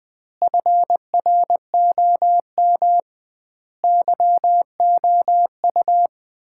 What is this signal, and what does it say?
Morse code, heard as a keyed tone, about 20 wpm, 700 Hz. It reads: FROM YOU
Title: Common QSO Phrases - 2x Word-Spacing - 20wpm